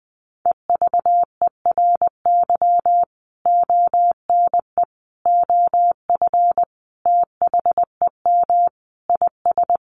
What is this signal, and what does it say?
Morse code, heard as a keyed tone, about 20 wpm, 700 Hz. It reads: EVERY ONE OF THEM IS
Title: Sentences from Top 1100 Words 20wpm